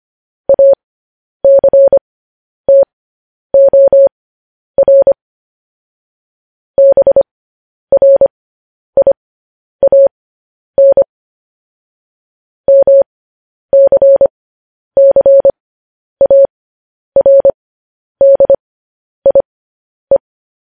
Morse code, heard as a keyed tone, about 25 wpm, spaced as for 10 wpm, 550 Hz. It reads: ACTOR BRIAN MCCARDIE